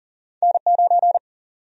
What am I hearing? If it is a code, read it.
Morse code, heard as a keyed tone, about 40 wpm, 700 Hz. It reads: N9
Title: USA Call Sign Prefixes - 40wpm